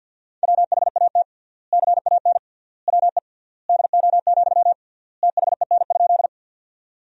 Morse code, heard as a keyed tone, about 53 wpm, 700 Hz. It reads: WHAT CAN WE DO, THEN?
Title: 3x Word-Spacing - Sentences from Top 100 Words - 50wpm